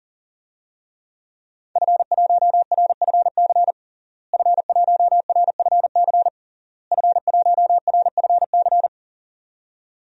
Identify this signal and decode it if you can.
Morse code, heard as a keyed tone, about 40 wpm, 700 Hz. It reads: F1RFC F1RFC F1RFC
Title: DXCC-100 Call Signs - 5 Characters - Repeated 3x - 40wpm